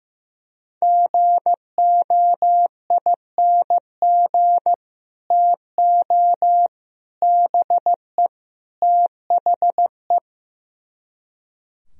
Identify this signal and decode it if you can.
Morse code, heard as a keyed tone, about 16 wpm, 700 Hz. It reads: GOING TO BE THE
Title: Speed-Racing - Sets of 4 Words 15wpm